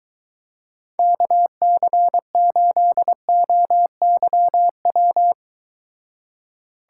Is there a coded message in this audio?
Yes: Morse code from 1.0 to 5.3 s, heard as a keyed tone, about 23 wpm, 700 Hz. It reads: KC8OYW